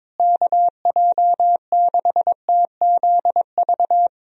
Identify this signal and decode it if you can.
Morse code, heard as a keyed tone, about 22 wpm, 700 Hz. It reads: KJ6TZV
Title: USA Call Signs - 6 Characters - Repeated 2x - 22wpm